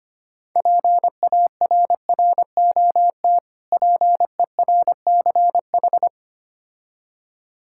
Morse code, heard as a keyed tone, about 25 wpm, 690 Hz. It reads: PARROT PERCH
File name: practice / 25wpm / Difficult Words with Hints - 25wpm